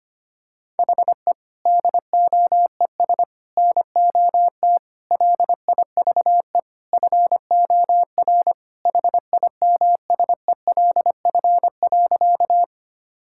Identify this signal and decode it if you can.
Morse code, heard as a keyed tone, about 25 wpm, 700 Hz. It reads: HE DOES NOT LIVE FOR HIMSELF.
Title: Sentences from Top 1600 Words 25wpm